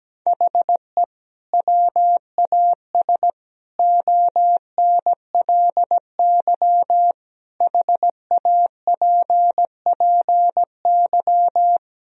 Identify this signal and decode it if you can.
Morse code, heard as a keyed tone, about 17 wpm, 700 Hz. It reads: HE WAS ONLY HAPPY
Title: Sentences from Top 1000 Words - Review Part 4 - 17wpm